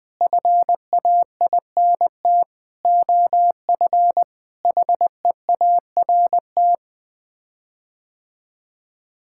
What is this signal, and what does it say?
Morse code, heard as a keyed tone, about 21 wpm, 700 Hz. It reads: FAINT OF HEART
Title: Sets of 6 Words - 20wpm